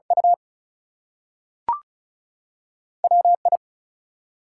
Morse code, heard as a keyed tone, about 36 wpm, spaced as for 6 wpm, 700 Hz. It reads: UWI
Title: US States and Canadian Provinces 35wpm